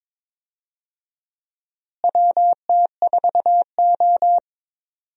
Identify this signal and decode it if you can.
Morse code, heard as a keyed tone, about 22 wpm, 700 Hz. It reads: WT4O
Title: USA Call Signs - 4 Characters - Repeated 2x - 22wpm